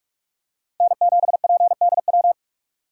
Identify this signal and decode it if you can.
Morse code, heard as a keyed tone, about 47 wpm, 700 Hz. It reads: N7PDW
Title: USA Call Signs - 5 Characters - 6x Character Spacing - 45wpm